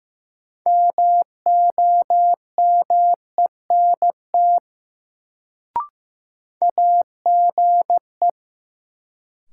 Morse code, heard as a keyed tone, about 16 wpm, 700 Hz. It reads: MOMENT AGE
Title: Sets of 3 Words - New Word Drill - 15wpm